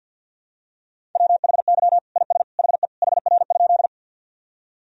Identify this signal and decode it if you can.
Morse code, heard as a keyed tone, about 51 wpm, 700 Hz. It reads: WHY IS HE HR?
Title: Sentences from Top 200 Words and Abbreviations 50wpm